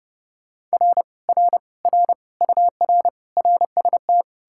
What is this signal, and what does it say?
Morse code, heard as a keyed tone, about 39 wpm, 700 Hz. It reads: R R R UR RST